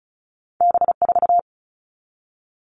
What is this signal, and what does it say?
Morse code, heard as a keyed tone, about 36 wpm, 700 Hz. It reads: B4
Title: All Cut Numbers - 2-Digit 35wpm